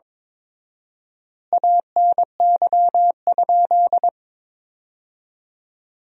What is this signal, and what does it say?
Morse code, heard as a keyed tone, about 22 wpm, 700 Hz. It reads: ANY?
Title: 4x Word-Spacing - Sentences from Top 300 Words - 22wpm